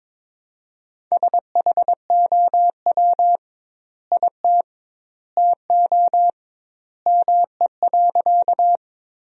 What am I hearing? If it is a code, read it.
Morse code, heard as a keyed tone, about 21 wpm, 700 Hz. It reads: SHOW IT TO ME.
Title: 2x Word-Spacing - Sentences from Top 200 Words - 22wpm